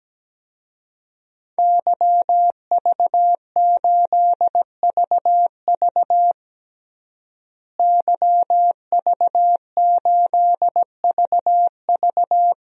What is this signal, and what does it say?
Morse code, heard as a keyed tone, about 17 wpm, 700 Hz. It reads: YV8VV YV8VV
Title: DXCC-100 Call Signs - 5 Characters - Repeated 2x - 17wpm